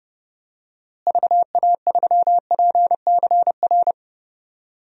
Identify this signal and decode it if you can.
Morse code, heard as a keyed tone, about 32 wpm, 700 Hz. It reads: VA3PCR
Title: DXCC-100 Call Signs - 6 Characters - Repeated 3x - 30wpm